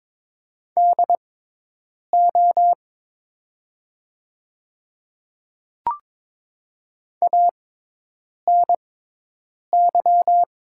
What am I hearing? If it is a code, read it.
Morse code, heard as a keyed tone, about 22 wpm, spaced as for 8 wpm, 690 Hz. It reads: DO ANY
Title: 6x Character Spacing - Top 300 Words - 22wpm